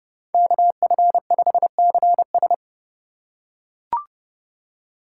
Morse code, heard as a keyed tone, about 31 wpm, 700 Hz. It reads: KF5CS
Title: USA Call Signs - 5 Characters - 3x Character Spacing - 30wpm